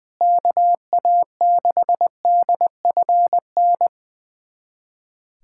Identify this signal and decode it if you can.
Morse code, heard as a keyed tone, about 20 wpm, 700 Hz. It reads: KA6DFN
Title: USA Callsign DE Callsign 20wpm